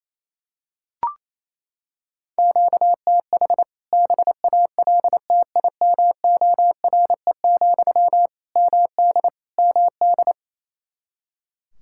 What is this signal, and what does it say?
Morse code, heard as a keyed tone, about 27 wpm, 700 Hz. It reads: QTH BALTIMORE, MD MD
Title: K1USN SST QSO Practice 28wpm